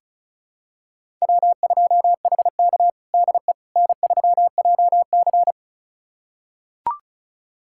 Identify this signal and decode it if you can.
Morse code, heard as a keyed tone, about 36 wpm, 700 Hz. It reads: W2HK DE N3JC
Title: USA Callsign DE Callsign - 4 Characters - 35wpm